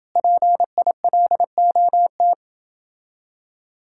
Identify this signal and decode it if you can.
Morse code, heard as a keyed tone, about 28 wpm, 700 Hz. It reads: PILOT